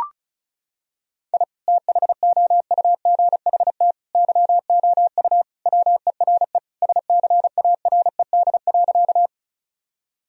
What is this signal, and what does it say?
Morse code, heard as a keyed tone, about 36 wpm, 700 Hz. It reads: I THOUGHT YOU WERE SCARED.
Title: Sentences from Top 1000 Words - Review Part 4 - 35wpm